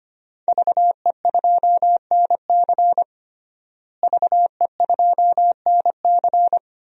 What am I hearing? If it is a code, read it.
Morse code, heard as a keyed tone, about 25 wpm, 700 Hz. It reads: VE2NC VE2NC
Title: DXCC-100 Call Signs - 5 Characters - Repeated 2x - 25wpm